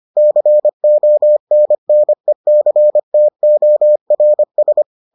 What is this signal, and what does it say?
Morse code, heard as a keyed tone, about 25 wpm, 600 Hz. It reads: CONNECTORS